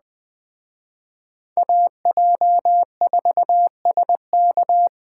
Morse code, heard as a keyed tone, about 20 wpm, 700 Hz. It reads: AJ4SK
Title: Speed Racing - USA Call Signs 17wpm